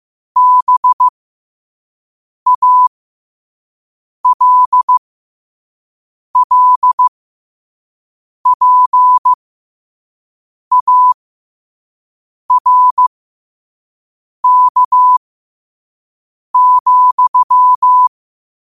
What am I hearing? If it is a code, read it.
Morse code, heard as a keyed tone, about 15 wpm, spaced as for 5 wpm, 1000 Hz. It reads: BALLPARK,
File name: 5wpm